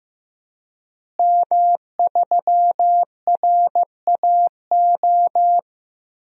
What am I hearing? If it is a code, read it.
Morse code, heard as a keyed tone, about 15 wpm, 700 Hz. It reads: M3RAO
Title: DXCC-100 Call Signs - 5 Characters - Repeated 3x - 15wpm